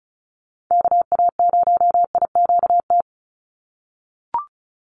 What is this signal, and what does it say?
Morse code, heard as a keyed tone, about 34 wpm, 700 Hz. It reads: KA0IQT
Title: Call Signs - 35wpm